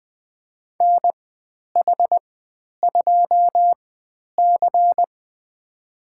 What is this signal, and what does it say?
Morse code, heard as a keyed tone, about 20 wpm, spaced as for 10 wpm, 700 Hz. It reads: NH2C
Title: 095 Lesson 16g - Letter c - Call Signs 20wpm